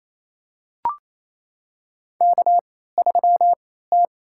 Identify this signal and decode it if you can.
Morse code, heard as a keyed tone, about 28 wpm, spaced as for 16 wpm, 700 Hz. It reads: K3T
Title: USA Call Signs - 3x Character Spacing - 28wpm